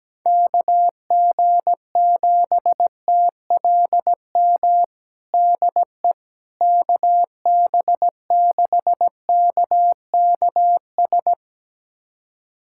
Morse code, heard as a keyed tone, about 17 wpm, 700 Hz. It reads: KG7TLM DE KB6KKS
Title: USA Callsign DE Callsign - 6 Characters - 17wpm